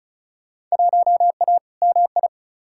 Morse code, heard as a keyed tone, about 34 wpm, 700 Hz. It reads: 1A MI
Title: Field Day Exchanges - 35wpm